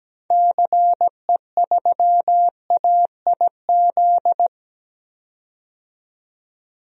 Morse code, heard as a keyed tone, about 17 wpm, 700 Hz. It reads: CE3AIZ
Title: DXCC-100 Call Signs - 6 Characters - 17wpm